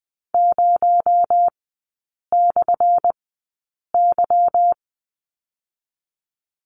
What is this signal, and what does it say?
Morse code, heard as a keyed tone, about 20 wpm, spaced as for 8 wpm, 700 Hz. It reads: 0/Y
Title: 218 Lesson 35e - Number 0 - Three Characters 20wpm